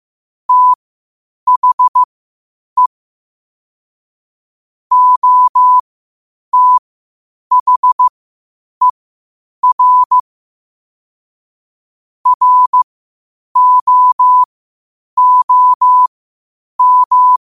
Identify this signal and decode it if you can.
Morse code, heard as a keyed tone, about 15 wpm, spaced as for 8 wpm, 1000 Hz. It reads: THE OTHER ROOM
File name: qotdfiles / 8wpm